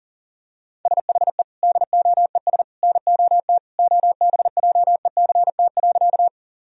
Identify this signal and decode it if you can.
Morse code, heard as a keyed tone, about 40 wpm, 700 Hz. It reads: SHE DOES NOT OBJECT.
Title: Sentences from Top 1000 Words - Review Part 3 - 40wpm